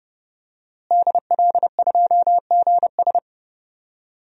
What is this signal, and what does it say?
Morse code, heard as a keyed tone, about 30 wpm, 700 Hz. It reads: DL2GS
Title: DXCC-100 Call Signs - 5 Characters - Repeated 3x - 30wpm